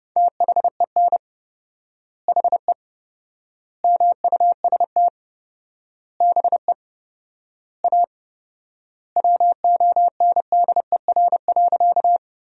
Morse code, heard as a keyed tone, about 31 wpm, 700 Hz. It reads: THEN HE MUST BE A WONDER.
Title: 4x Word-Spacing - Sentences from Top 500 Words - 30wpm